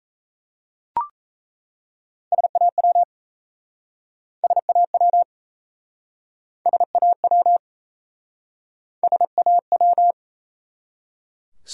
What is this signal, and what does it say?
Morse code, heard as a keyed tone, about 37 wpm, 700 Hz. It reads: SAW SAW SAW SAW